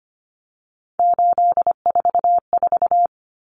Morse code, heard as a keyed tone, about 25 wpm, 700 Hz. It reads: 844
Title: Common Cut Numbers - 3-Digit 25wpm